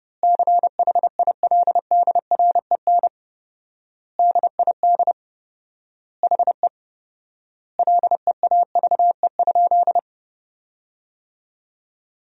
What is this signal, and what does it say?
Morse code, heard as a keyed tone, about 30 wpm, 700 Hz. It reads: CHILDREN DID HE LEAVE?
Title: 4x Word-Spacing - Sentences from Top 400 Words - 30wpm